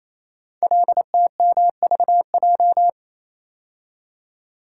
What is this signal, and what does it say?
Morse code, heard as a keyed tone, about 28 wpm, 700 Hz. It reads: LTMVJ
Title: Code Groups - 5 Letters - 28wpm